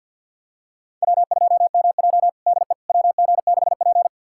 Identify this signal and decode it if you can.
Morse code, heard as a keyed tone, about 49 wpm, 700 Hz. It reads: W2MJ DE WG6P
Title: USA Callsign DE Callsign - 4 Characters - 50wpm